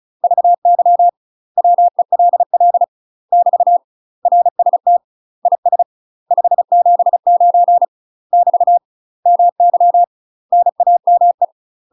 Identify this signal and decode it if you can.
Morse code, heard as a keyed tone, about 34 wpm, 700 Hz. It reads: VY WELL <BT> RST IS 579 <BT> MY NAME
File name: UK MASTER QSOs TRAINING / 35WPM / UK MASTER QSOs 0042